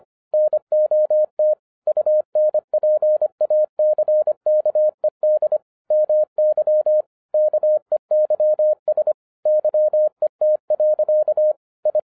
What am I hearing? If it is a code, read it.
Morse code, heard as a keyed tone, about 25 wpm, 610 Hz. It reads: NOT UNPACKED MY KEYS YET. I
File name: FIRST FISTS CONTACT-25wpm